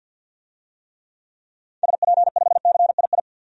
Morse code, heard as a keyed tone, about 48 wpm, 700 Hz. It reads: SP5CII
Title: DXCC-100 Call Signs - 6 Characters - 50wpm